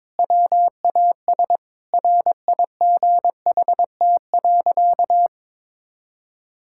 Morse code, heard as a keyed tone, about 22 wpm, 700 Hz. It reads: WAS RIGHT.
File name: practice / 22wpm / Sentences with 4-Word Phrases - 22wpm